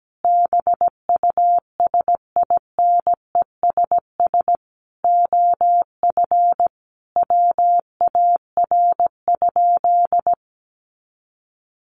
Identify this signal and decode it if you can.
Morse code, heard as a keyed tone, about 17 wpm, 700 Hz. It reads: BUSINESS OF WAR?
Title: Sentences from Top 1300 Words 17wpm